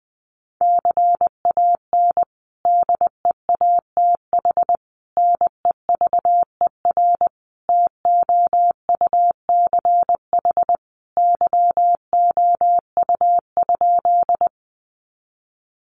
Morse code, heard as a keyed tone, about 20 wpm, 700 Hz. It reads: CAN DEATH NEVER TOUCH YOU?
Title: Sentences from Top 1200 Words 20wpm